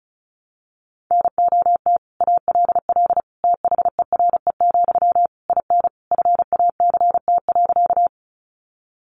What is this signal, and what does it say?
Morse code, heard as a keyed tone, about 34 wpm, 700 Hz. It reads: NOT ALL THERE, IN FACT.
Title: Sentences from Top 600 Words 35wpm